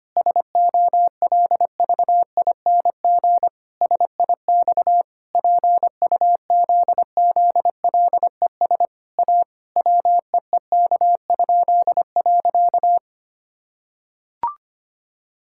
Morse code, heard as a keyed tone, about 25 wpm, 700 Hz. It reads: SOLVING SIX PUZZLES A WEEK?.
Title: Pangram Challenge v2 - 25wpm